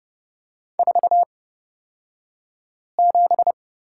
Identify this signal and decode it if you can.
Morse code, heard as a keyed tone, about 30 wpm, spaced as for 5 wpm, 700 Hz. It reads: 47
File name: Lesson 36d - BT - Two Characters 30wpm